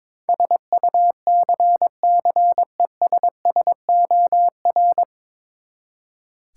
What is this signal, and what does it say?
Morse code, heard as a keyed tone, about 22 wpm, 700 Hz. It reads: SUCCESSOR
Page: Morse Code Ninja practice 22wpm